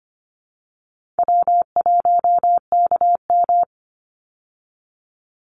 Morse code, heard as a keyed tone, about 24 wpm, 700 Hz. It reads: W1KM
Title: Call Signs Repeated 2x - 25wpm